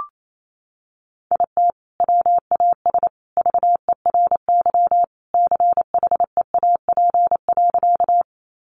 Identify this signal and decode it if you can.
Morse code, heard as a keyed tone, about 28 wpm, 700 Hz. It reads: IT WAS VERY CHEAP.
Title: Sentences from Top 1200 Words 28wpm